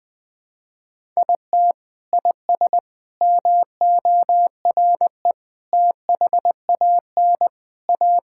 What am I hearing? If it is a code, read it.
Morse code, heard as a keyed tone, about 21 wpm, 700 Hz. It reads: IT IS MORE THAN A
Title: Sentences with 3-Word Phrases - Spoken Beginning - 20wpm